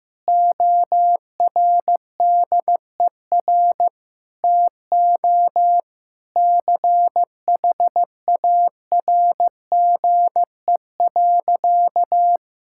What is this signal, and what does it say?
Morse code, heard as a keyed tone, about 15 wpm, 700 Hz. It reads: ORDER TO CHARGE.
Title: Sentences from Top 1000 Words - Review Part 6 - 15wpm